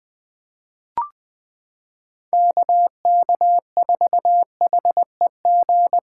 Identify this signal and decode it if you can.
Morse code, heard as a keyed tone, about 20 wpm, 700 Hz. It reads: KK4HEG
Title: K1USN SST QSO Practice - 20wpm